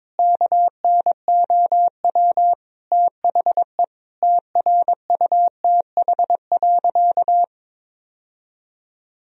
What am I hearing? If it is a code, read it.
Morse code, heard as a keyed tone, about 22 wpm, 700 Hz. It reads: KNOW THE TRUTH.
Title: Sentences from Top 1000 Words - Review Part 3 - 22wpm